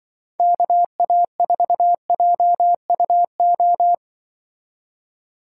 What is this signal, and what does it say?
Morse code, heard as a keyed tone, about 23 wpm, 700 Hz. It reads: KA4JUO
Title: Speed Racing - USA 2x3 Call Signs 20wpm